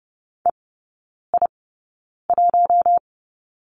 Morse code, heard as a keyed tone, about 30 wpm, spaced as for 9 wpm, 700 Hz. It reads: EI1
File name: 081 Lesson 14e - Number 2 - Three Characters 30wpm